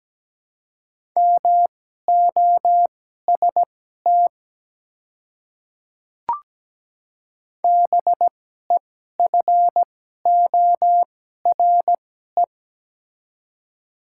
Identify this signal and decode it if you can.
Morse code, heard as a keyed tone, about 17 wpm, spaced as for 12 wpm, 700 Hz. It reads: MOST BEFORE
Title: 2x Character Spacing - Top 200 Words - 17wpm